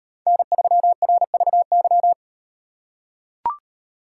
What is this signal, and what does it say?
Morse code, heard as a keyed tone, about 38 wpm, 700 Hz. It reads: N3RVY